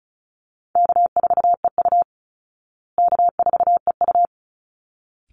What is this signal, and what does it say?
Morse code, heard as a keyed tone, about 34 wpm, 700 Hz. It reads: K4EU K4EU
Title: Easy Call Signs Repeated 2x - 35wpm